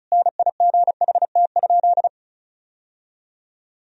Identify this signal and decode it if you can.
Morse code, heard as a keyed tone, about 34 wpm, 700 Hz. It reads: NIGHT?
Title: Sentences from Top 1000 Words - Review Part 4 - 35wpm